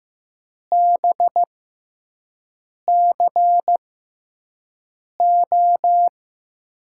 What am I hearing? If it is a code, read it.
Morse code, heard as a keyed tone, about 15 wpm, spaced as for 5 wpm, 700 Hz. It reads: BCO
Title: USA Call Signs - 6 Characters - 6x Character Spacing - 15wpm